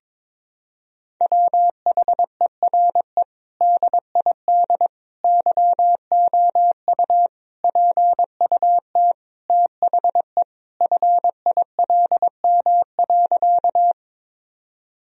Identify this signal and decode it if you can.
Morse code, heard as a keyed tone, about 22 wpm, 700 Hz. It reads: WHERE DID YOU PUT THE FILM.
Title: Sentences from Top 1000 Words - Review Part 3 - 22wpm